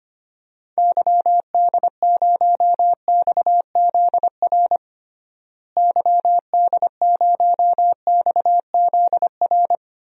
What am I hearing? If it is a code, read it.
Morse code, heard as a keyed tone, about 25 wpm, 700 Hz. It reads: YD0XZR YD0XZR
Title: DXCC-100 Call Signs - 6 Characters - Repeated 2x - 25wpm